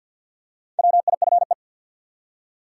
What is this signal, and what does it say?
Morse code, heard as a keyed tone, about 51 wpm, 700 Hz. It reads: WIFE
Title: Four Letter Words - 50wpm